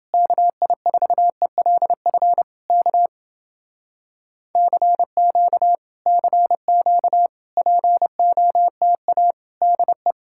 Morse code, heard as a keyed tone, about 30 wpm, 700 Hz. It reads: KI4ELF K CQ CQ POTA DE